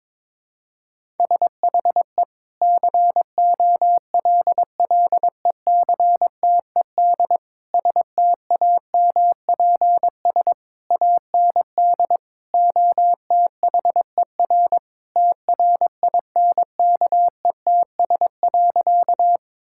Morse code, heard as a keyed tone, about 22 wpm, 700 Hz. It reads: SHE COLLECTED STAMPS AND OTHER TRINKETS.
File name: practice / 22wpm / Sentences with 2-Word Phrases - 22wpm